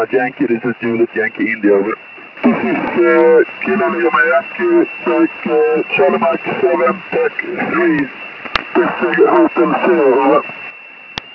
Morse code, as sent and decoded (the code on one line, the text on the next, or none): none